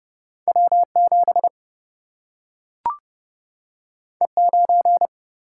Morse code, heard as a keyed tone, about 30 wpm, 700 Hz. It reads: W7 E9
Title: Single-Letter Single-Number 30wpm